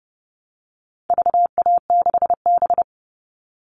Morse code, heard as a keyed tone, about 30 wpm, 700 Hz. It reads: VA6B
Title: All Cut Numbers - 4-Digit 30wpm